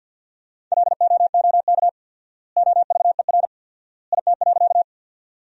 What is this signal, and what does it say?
Morse code, heard as a keyed tone, about 50 wpm, 700 Hz. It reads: LOOK OVER IT.
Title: 4x Word-Spacing - Sentences from Top 100 Words - 50wpm